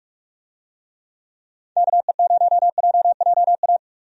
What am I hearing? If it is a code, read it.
Morse code, heard as a keyed tone, about 46 wpm, 700 Hz. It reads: KE0JJA
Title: USA Call Signs - 6 Characters - 45wpm